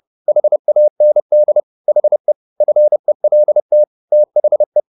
Morse code, heard as a keyed tone, about 30 wpm, 600 Hz. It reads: HAND HE FELT THE
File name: the old man and the sea / the old man and the sea 1 30wpm